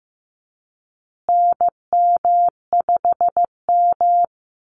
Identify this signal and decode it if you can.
Morse code, heard as a keyed tone, about 15 wpm, 700 Hz. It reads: NM5M
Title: Call Signs Repeated 2x - 15wpm